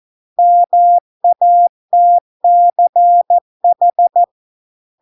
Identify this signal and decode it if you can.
Morse code, heard as a keyed tone, about 14 wpm, 700 Hz. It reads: MATCH